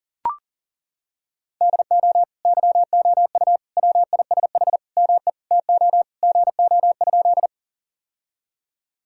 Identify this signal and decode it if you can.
Morse code, heard as a keyed tone, about 40 wpm, 700 Hz. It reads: DO YOU WISH ME TO GO?
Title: Sentences from Top 1600 Words 40wpm